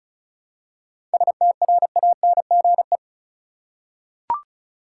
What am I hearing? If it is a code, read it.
Morse code, heard as a keyed tone, about 34 wpm, 700 Hz. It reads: STRANGE
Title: Seven Letter Words - 35wpm